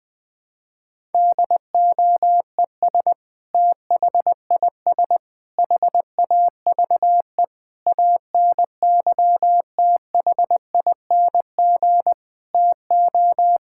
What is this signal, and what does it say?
Morse code, heard as a keyed tone, about 21 wpm, 700 Hz. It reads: DOES THIS HAVE ANYTHING TO
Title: Sentences with 5-Word Phrases - Spoken Beginning - 20wpm